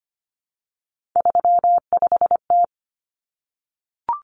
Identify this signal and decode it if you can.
Morse code, heard as a keyed tone, about 25 wpm, 700 Hz. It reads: 35T
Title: Common Cut Numbers - 3-Digit 25wpm